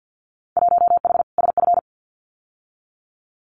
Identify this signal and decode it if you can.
Morse code, heard as a keyed tone, about 50 wpm, 700 Hz. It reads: 1H SF
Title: Winter Field Day Exchanges 50wpm